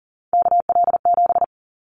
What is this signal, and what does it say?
Morse code, heard as a keyed tone, about 42 wpm, 700 Hz. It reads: KL7
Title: Callsign Prefixes for Top 100 DXCC - 40wpm